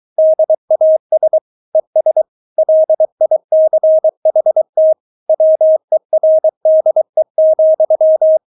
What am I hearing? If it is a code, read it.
Morse code, heard as a keyed tone, about 23 wpm, 630 Hz. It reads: DAS ES LICHT WERDE,
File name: Witz4 23wpm